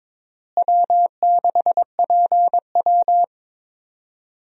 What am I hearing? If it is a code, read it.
Morse code, heard as a keyed tone, about 21 wpm, 700 Hz. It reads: W6PW
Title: USA Call Signs - 4 Characters - Repeated 2x - 22wpm